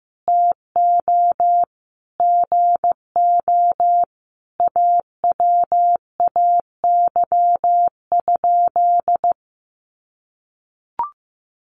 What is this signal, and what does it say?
Morse code, heard as a keyed tone, about 14 wpm, 700 Hz. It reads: TO GO AWAY?
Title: Sentences from Top 600 Words 15wpm